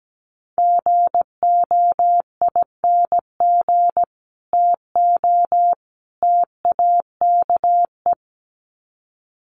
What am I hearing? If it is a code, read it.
Morse code, heard as a keyed tone, about 17 wpm, 700 Hz. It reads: GOING TO TAKE
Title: Sets of 3 Words 17wpm